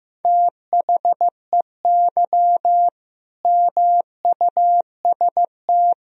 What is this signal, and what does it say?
Morse code, heard as a keyed tone, about 15 wpm, 700 Hz. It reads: THEY MUST
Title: Sentences from Top 1000 Words - Review Part 4 - 15wpm